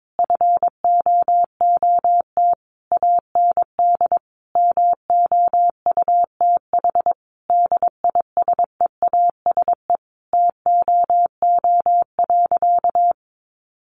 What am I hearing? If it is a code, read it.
Morse code, heard as a keyed tone, about 22 wpm, 700 Hz. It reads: FOOT AND MOUTH DISEASE TOO.
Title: Sentences from Top 1500 Words 22wpm